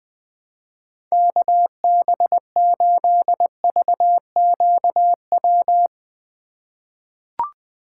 Morse code, heard as a keyed tone, about 20 wpm, 700 Hz. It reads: KB8VQW
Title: USA Call Signs - 6 Characters - 4x Character Spacing - 20wpm